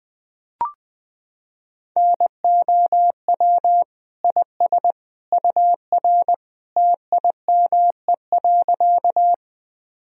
Morse code, heard as a keyed tone, about 20 wpm, 700 Hz. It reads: NOW IS UR TIME.
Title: Sentences from Top 100 Words and Abbreviations 20wpm